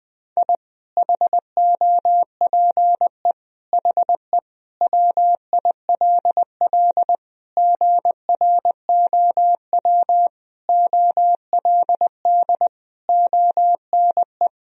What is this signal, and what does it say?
Morse code, heard as a keyed tone, about 20 wpm, 700 Hz. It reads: I HOPE HE WILL GROW OLD ONE
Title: Sentences from Top 1000 Words - Review Part 1 - 20wpm